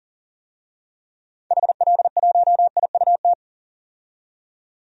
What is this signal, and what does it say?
Morse code, heard as a keyed tone, about 40 wpm, 700 Hz. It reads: HL1IUT
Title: DXCC-100 Call Signs - 6 Characters - 40wpm